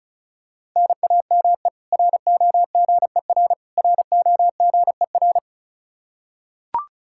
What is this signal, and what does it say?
Morse code, heard as a keyed tone, about 34 wpm, 700 Hz. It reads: NAME ROGER ROGER
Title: Practice QSO - Exchange 1 - Beginning - 35wpm